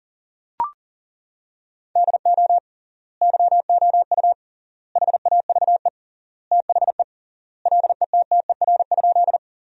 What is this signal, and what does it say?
Morse code, heard as a keyed tone, about 37 wpm, 700 Hz. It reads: DO YOU HAVE THE LETTER?
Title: 3x Word-Spacing - Sentences from Top 300 Words - 40wpm